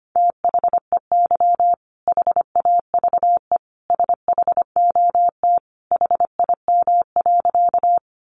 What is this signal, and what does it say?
Morse code, heard as a keyed tone, about 25 wpm, 700 Hz. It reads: THEY HAVE SHOT HIM.
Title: Sentences from Top 1100 Words 25wpm